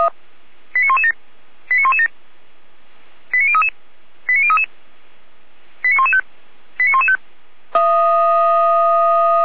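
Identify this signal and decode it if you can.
Five-tone selective call (ZVEI), heard as quick runs of five tones, tones of about 80 ms. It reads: 89117 89117 89033 89033 89115 89115
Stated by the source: ZVEI five-tone call 89080 received by SDR.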